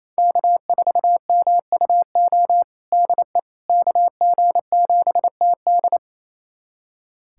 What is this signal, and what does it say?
Morse code, heard as a keyed tone, about 28 wpm, 700 Hz. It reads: K4MUO DE KG7TD
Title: USA Callsign DE Callsign - 5 Characters - 28wpm